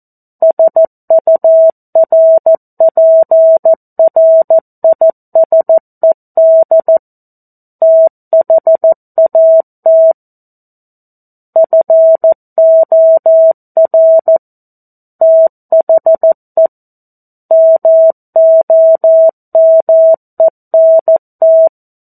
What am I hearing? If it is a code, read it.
Morse code, heard as a keyed tone, about 14 wpm, 650 Hz. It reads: SURPRISED THAT FOR THE MOMENT
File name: 02-Chapter-02 14 WPM